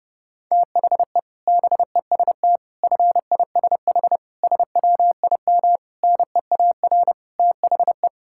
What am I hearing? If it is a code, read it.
Morse code, heard as a keyed tone, about 32 wpm, 700 Hz. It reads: THE BEST FISH SWIM NEAR THE
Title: Sentences from Top 1000 Words - Review Part 6 - 30wpm